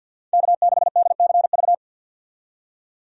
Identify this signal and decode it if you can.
Morse code, heard as a keyed tone, about 52 wpm, 700 Hz. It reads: K6DCV
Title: USA Call Signs - 5 Characters - 50wpm